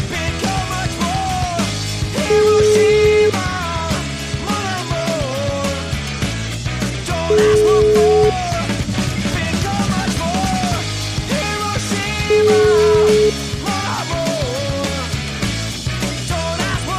Ringback tone (Europe, 425 Hz), with noise.